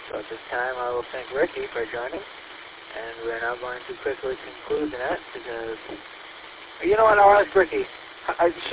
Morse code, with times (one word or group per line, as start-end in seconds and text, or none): none